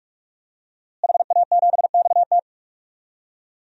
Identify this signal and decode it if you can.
Morse code, heard as a keyed tone, about 46 wpm, 700 Hz. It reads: HA7XT
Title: DXCC-100 Call Signs - 5 Characters - Repeated 2x - 45wpm